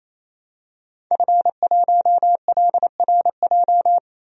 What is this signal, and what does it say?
Morse code, heard as a keyed tone, about 27 wpm, 700 Hz. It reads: F1LRJ
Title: DXCC-100 Call Signs - 5 Characters - 28wpm